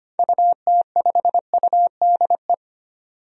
Morse code, heard as a keyed tone, about 25 wpm, 700 Hz. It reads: UT5UDE
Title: DXCC-100 Call Signs - 6 Characters - 25wpm